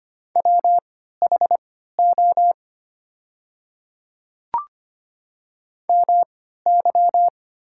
Morse code, heard as a keyed tone, about 25 wpm, spaced as for 14 wpm, 700 Hz. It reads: WHO MY
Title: 3x Character Spacing - Top 100 Words - 25wpm